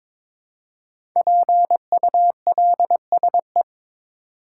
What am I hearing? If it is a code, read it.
Morse code, heard as a keyed tone, about 22 wpm, 700 Hz. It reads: PULSE
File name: 5 Letter Words - First 3 Letters Given - 22wpm